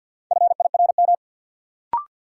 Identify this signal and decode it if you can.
Morse code, heard as a keyed tone, about 52 wpm, 700 Hz. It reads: FIRM
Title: Sentences from Top 1400 Words - New Word Drill - 50wpm